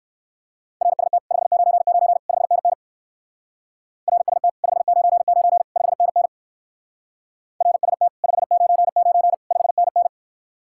Morse code, heard as a keyed tone, about 58 wpm, 700 Hz. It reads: RST 599 5NN RST 599 5NN RST 599 5NN